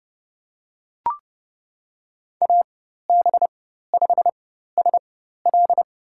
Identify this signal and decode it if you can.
Morse code, heard as a keyed tone, about 30 wpm, spaced as for 14 wpm, 700 Hz. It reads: AB5SL
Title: USA Call Signs - 5 Characters - 4x Character Spacing - 30wpm